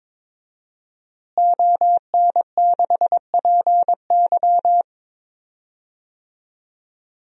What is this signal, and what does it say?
Morse code, heard as a keyed tone, about 22 wpm, 700 Hz. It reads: ON6PY
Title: DXCC-100 Call Signs - 5 Characters - Repeated 2x - 22wpm